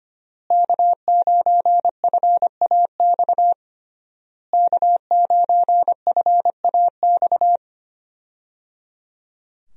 Morse code, heard as a keyed tone, about 25 wpm, 700 Hz. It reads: K9FAX K9FAX
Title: USA Call Signs - 5 Characters - Repeated 3x - 25wpm